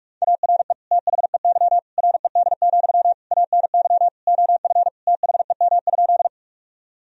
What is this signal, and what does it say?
Morse code, heard as a keyed tone, about 46 wpm, 700 Hz. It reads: ARE THEY RED, ANY OF THEM?
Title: Sentences from Top 1000 Words - Review Part 2 - 45wpm